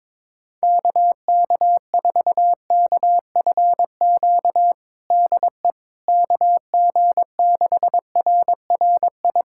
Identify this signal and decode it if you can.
Morse code, heard as a keyed tone, about 22 wpm, 700 Hz. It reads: KK4KFQ DE KG6RRI
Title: USA Callsign DE Callsign - 6 Characters - 22wpm